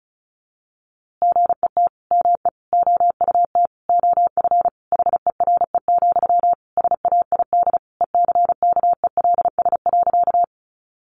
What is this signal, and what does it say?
Morse code, heard as a keyed tone, about 34 wpm, 700 Hz. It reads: GET ME OUT OF HERE, SAID ECKELS.
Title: A Sound Of Thunder 35wpm